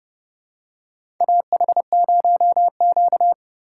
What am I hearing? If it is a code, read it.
Morse code, heard as a keyed tone, about 31 wpm, 700 Hz. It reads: AH0Q
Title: USA Call Signs - 4 Characters - 4x Character Spacing - 30wpm